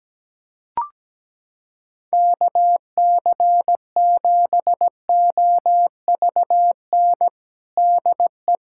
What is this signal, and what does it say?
Morse code, heard as a keyed tone, about 17 wpm, 700 Hz. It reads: KC7OVN DE
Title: USA Callsign DE Callsign 17wpm